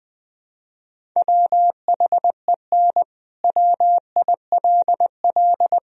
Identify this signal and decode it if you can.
Morse code, heard as a keyed tone, about 21 wpm, 700 Hz. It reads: WHEN WILL
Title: Sentences from Top 1000 Words - Review Part 1 - 20wpm